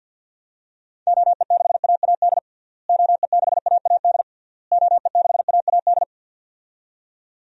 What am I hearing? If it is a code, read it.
Morse code, heard as a keyed tone, about 49 wpm, 700 Hz. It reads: OE6AAD OE6AAD OE6AAD
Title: DXCC-100 Call Signs - 6 Characters - Repeated 3x - 50wpm